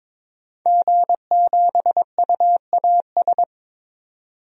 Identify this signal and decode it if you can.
Morse code, heard as a keyed tone, about 22 wpm, 700 Hz. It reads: G7UAS
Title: DXCC-100 Call Signs - 5 Characters - 22wpm